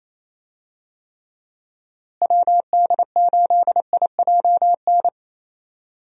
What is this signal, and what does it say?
Morse code, heard as a keyed tone, about 28 wpm, 700 Hz. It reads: WD8IJN
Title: USA Call Signs - 6 Characters - Repeated 2x - 28wpm